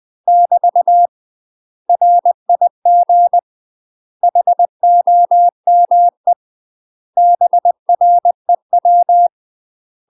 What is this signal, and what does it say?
Morse code, heard as a keyed tone, about 20 wpm, 700 Hz. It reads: <BT> RIG HOME BREW
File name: UK MASTER QSOs TRAINING / 20WPM / UK MASTER QSOs 0024